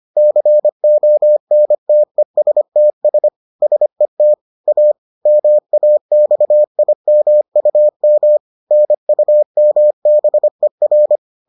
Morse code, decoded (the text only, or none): CONTESTS SET A MAXIMUM NUMBER